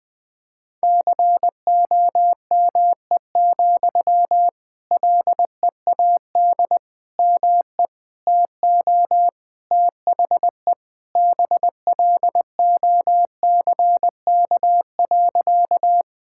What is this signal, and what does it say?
Morse code, heard as a keyed tone, about 21 wpm, 700 Hz. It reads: COME, LEAD ME TO THE BLOCK.
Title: Sentences from Top 1600 Words 20wpm